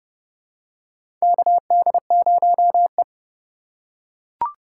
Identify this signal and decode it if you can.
Morse code, heard as a keyed tone, about 29 wpm, 700 Hz. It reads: KD0E
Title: USA Call Signs - 4 Characters - 6x Character Spacing - 30wpm